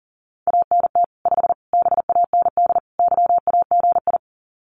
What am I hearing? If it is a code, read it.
Morse code, heard as a keyed tone, about 40 wpm, 700 Hz. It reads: ANT 5 BAND YAGI
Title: Common QSO Phrases 40wpm